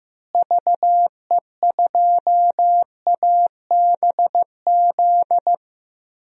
Morse code, heard as a keyed tone, about 16 wpm, 700 Hz. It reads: VE2ABZ
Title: DXCC-100 Call Signs - 6 Characters - Repeated 2x - 15wpm